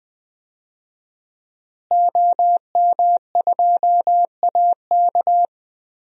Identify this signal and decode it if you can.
Morse code, heard as a keyed tone, about 20 wpm, 700 Hz. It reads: OM2AK
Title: DXCC-100 Call Signs - 5 Characters - Repeated 2x - 20wpm